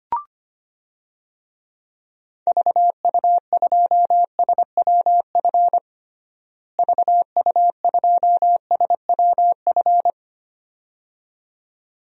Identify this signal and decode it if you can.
Morse code, heard as a keyed tone, about 25 wpm, 700 Hz. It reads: VU2SWF VU2SWF
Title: DXCC-100 Call Signs - 6 Characters - Repeated 2x - 25wpm